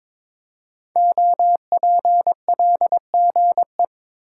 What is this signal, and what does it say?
Morse code, heard as a keyed tone, about 22 wpm, 700 Hz. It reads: OPLGE
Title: Code Groups - 5 Letters - 22wpm